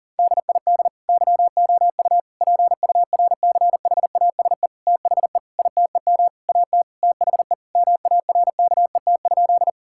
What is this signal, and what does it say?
Morse code, heard as a keyed tone, about 37 wpm, 700 Hz. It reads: DID YOU PURCHASE THE ITEM AT THE MARKET?
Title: Sentences from Top 1000 Words - Review Part 6 - 40wpm